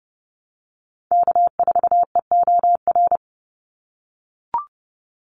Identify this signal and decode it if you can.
Morse code, heard as a keyed tone, about 30 wpm, 700 Hz. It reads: K4EOR
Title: Call Signs - 30wpm